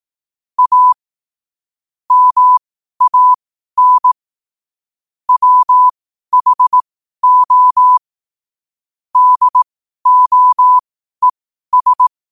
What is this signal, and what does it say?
Morse code, heard as a keyed tone, about 18 wpm, spaced as for 12 wpm, 1000 Hz. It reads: A MAN WHO DOES
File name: qotdfiles / 12wpm